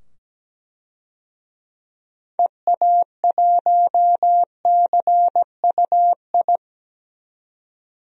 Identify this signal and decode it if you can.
Morse code, heard as a keyed tone, about 17 wpm, 700 Hz. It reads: EA1CUI
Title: DXCC-100 Call Signs - 6 Characters - 17wpm